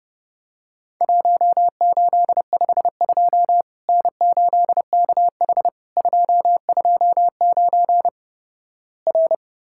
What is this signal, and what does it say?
Morse code, heard as a keyed tone, about 30 wpm, 700 Hz. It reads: 1852 N8KH 229 R
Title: Worked All Europe DX Contest - QTC Exchange - 30wpm